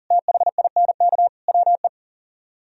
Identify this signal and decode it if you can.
Morse code, heard as a keyed tone, about 55 wpm, spaced as for 48 wpm, 700 Hz. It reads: THINK WE
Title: Speed-Racing - Sets of 4 Words 40wpm